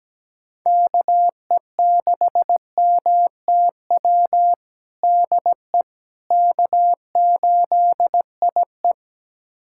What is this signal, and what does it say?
Morse code, heard as a keyed tone, about 17 wpm, 700 Hz. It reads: KE6MTW DE K8IE
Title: USA Callsign DE Callsign 17wpm